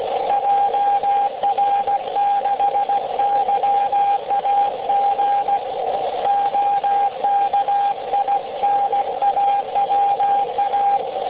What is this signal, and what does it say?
Morse code, heard as a keyed tone, about 16 wpm, 800 Hz, with radio noise. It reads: JR6YAG OKINAWA